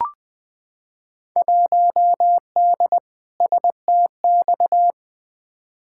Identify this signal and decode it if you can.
Morse code, heard as a keyed tone, about 20 wpm, 700 Hz. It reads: 1D STX
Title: Field Day Exchanges - 20wpm